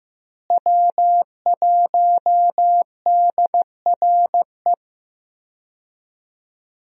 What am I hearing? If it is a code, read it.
Morse code, heard as a keyed tone, about 16 wpm, 700 Hz. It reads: W1DRE
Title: USA Call Signs - 5 Characters - Repeated 2x - 15wpm